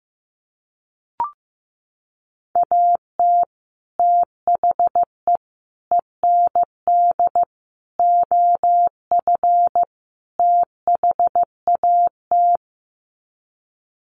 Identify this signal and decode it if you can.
Morse code, heard as a keyed tone, about 15 wpm, 700 Hz. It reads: AT THE END OF THAT
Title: Sets of 5 Words Encore2 15wpm